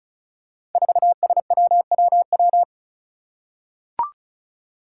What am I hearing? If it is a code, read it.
Morse code, heard as a keyed tone, about 36 wpm, 700 Hz. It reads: 4SWWW
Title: Five Challenging Letter-Number Combinations 35wpm